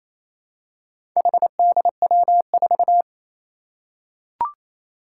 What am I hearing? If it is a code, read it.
Morse code, heard as a keyed tone, about 27 wpm, 700 Hz. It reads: HDW4
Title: Four Challenging Letter-Number Combinations 28wpm